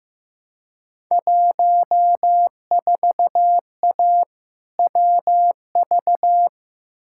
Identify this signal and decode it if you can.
Morse code, heard as a keyed tone, about 15 wpm, 700 Hz. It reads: 14A WV
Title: Field Day Exchanges Encore - 15wpm